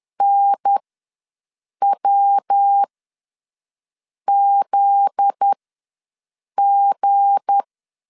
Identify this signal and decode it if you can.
Morse code, heard as a keyed tone, about 10 wpm, spaced as for 5 wpm, 800 Hz. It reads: NWZG
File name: LEZIONI / 7 GF9Z / 7 GF9Z ws 10wpm 3919